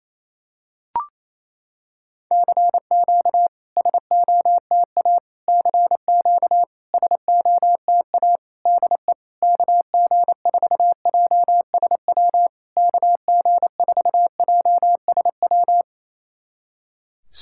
Morse code, heard as a keyed tone, about 28 wpm, 700 Hz. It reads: CQ SOTA CQ SOTA DE KG4JSW KG4JSW
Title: SOTA QSO Practice 28wpm